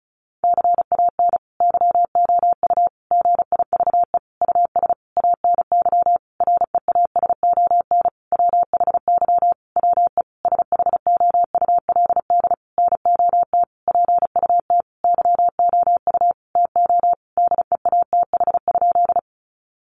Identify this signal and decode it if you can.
Morse code, heard as a keyed tone, about 34 wpm, 700 Hz. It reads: CAN YOU GIVE US ANY REASON WHY WE SHOULD NOT PUT YOU TO DEATH?
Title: Sentences from Top 1100 Words 35wpm